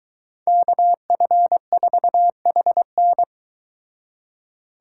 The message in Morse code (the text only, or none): KF4HN